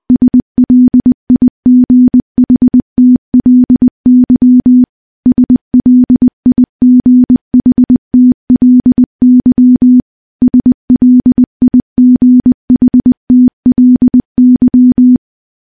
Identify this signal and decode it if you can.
Morse code, heard as a keyed tone, about 20 wpm, 250 Hz. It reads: SLIGHTLY SLIGHTLY SLIGHTLY
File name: morse / morse-training-20WPM